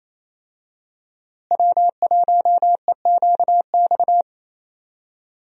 Morse code, heard as a keyed tone, about 27 wpm, 700 Hz. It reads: W1EQX
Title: USA Call Signs - 5 Characters - Repeated 3x - 28wpm